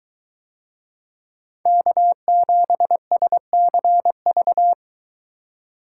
Morse code, heard as a keyed tone, about 22 wpm, 700 Hz. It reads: K7SCV